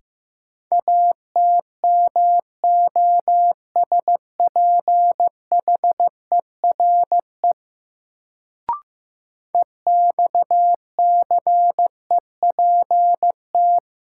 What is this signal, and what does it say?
Morse code, heard as a keyed tone, about 15 wpm, 700 Hz. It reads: ATMOSPHERE EXCEPT
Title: Sentences from Top 1600 Words - New Word Drill - 15wpm